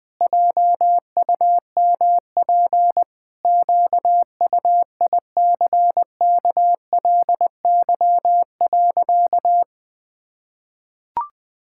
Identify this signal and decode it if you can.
Morse code, heard as a keyed tone, about 20 wpm, 700 Hz. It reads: JUMP QUICKLY.
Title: Pangram Challenge v2 - 20wpm